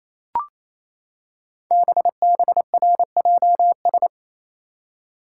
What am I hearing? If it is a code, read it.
Morse code, heard as a keyed tone, about 28 wpm, 700 Hz. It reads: BBRJS
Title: Code Groups - 5 Letters - 28wpm